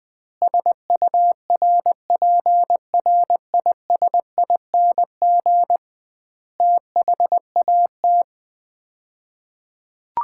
Morse code, heard as a keyed tone, about 20 wpm, 700 Hz. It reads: SURPRISING THAT
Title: 2x Word-Spacing - Sets of 5 Words - 20wpm